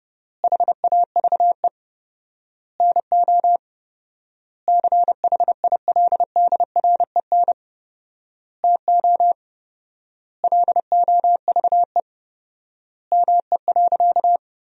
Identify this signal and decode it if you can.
Morse code, heard as a keyed tone, about 30 wpm, 700 Hz. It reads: HAVE NO CHILDREN TO LOVE ME.
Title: 4x Word-Spacing - Sentences from Top 400 Words - 30wpm